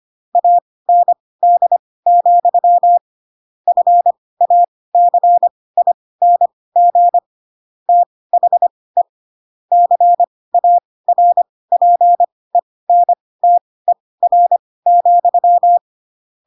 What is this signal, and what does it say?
Morse code, heard as a keyed tone, about 25 wpm, spaced as for 18 wpm, 700 Hz. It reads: AND, FACING THE CARPENTER,
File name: Pinocchio / Pinocchio ch2 12 25 ik1hge